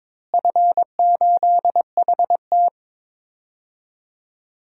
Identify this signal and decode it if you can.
Morse code, heard as a keyed tone, about 22 wpm, 700 Hz. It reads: F8HT
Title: DXCC-100 Call Signs - 4 Characters - Repeated 2x - 22wpm